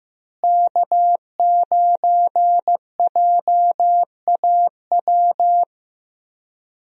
Morse code, heard as a keyed tone, about 15 wpm, 700 Hz. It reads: K9JAW
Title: USA Call Signs - 5 Characters - Repeated 3x - 15wpm